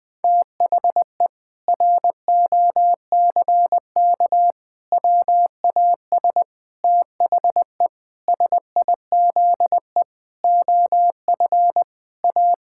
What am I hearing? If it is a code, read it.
Morse code, heard as a keyed tone, about 20 wpm, 700 Hz. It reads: THE ROCK WAS THE SIZE OF A
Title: Sentences with 4-Word Phrases - Spoken Beginning - 20wpm